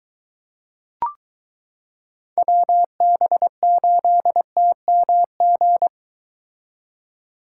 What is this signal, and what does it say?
Morse code, heard as a keyed tone, about 22 wpm, 700 Hz. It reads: WB8TMG